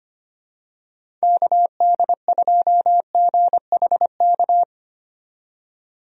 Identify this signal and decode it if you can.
Morse code, heard as a keyed tone, about 25 wpm, 700 Hz. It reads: KD2GHK
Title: USA Call Signs - 6 Characters - 25wpm